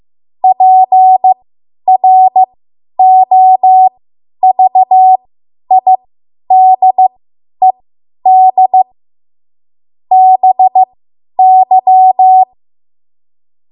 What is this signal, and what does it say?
Morse code, heard as a keyed tone, about 15 wpm, spaced as for 10 wpm, 750 Hz. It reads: PROVIDED BY
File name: Morse / 10WPM